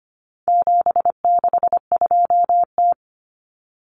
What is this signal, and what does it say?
Morse code, heard as a keyed tone, about 25 wpm, 700 Hz. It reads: 762T
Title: Common Cut Numbers - 4-Digit 25wpm